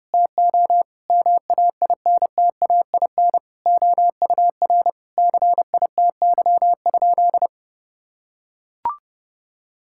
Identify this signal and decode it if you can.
Morse code, heard as a keyed tone, about 28 wpm, 700 Hz. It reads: TO MAINTAIN OUR CITY?
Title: Sentences from Top 1000 Words - Review Part 5 - 30wpm